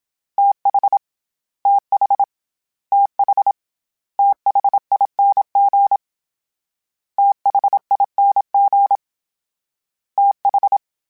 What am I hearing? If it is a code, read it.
Morse code, heard as a keyed tone, about 27 wpm, 800 Hz. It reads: TH TH TH THING THING TH
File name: TH / TH mp3s / TH Prefixes 3x2 @25